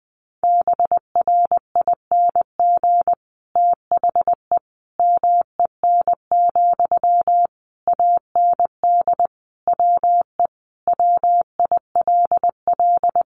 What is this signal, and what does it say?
Morse code, heard as a keyed tone, about 20 wpm, 700 Hz. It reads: BRING THE MEN, AND WE WILL
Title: Sentences from Top 1400 Words 20wpm